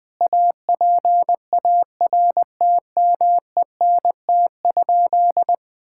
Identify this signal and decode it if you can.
Morse code, heard as a keyed tone, about 20 wpm, 700 Hz. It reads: APARTMENT?
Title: Sentences from Top 1000 Words - Review Part 6 - 20wpm